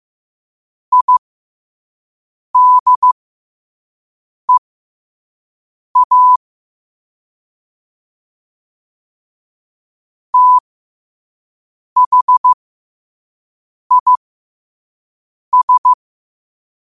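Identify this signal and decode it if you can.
Morse code, heard as a keyed tone, about 15 wpm, spaced as for 5 wpm, 1000 Hz. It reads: IDEA THIS
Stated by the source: Morse code practice at 5 wpm.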